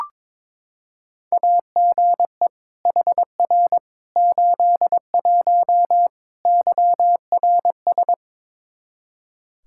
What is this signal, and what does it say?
Morse code, heard as a keyed tone, about 22 wpm, 700 Hz. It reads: AGE HR 81 YRS
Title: Practice QSO - Exchange 3 - More Info - 22wpm